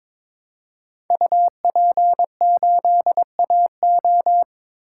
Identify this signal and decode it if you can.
Morse code, heard as a keyed tone, about 21 wpm, 700 Hz. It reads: UP8AO
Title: Code Groups - 5 Letter:Numbers - 22wpm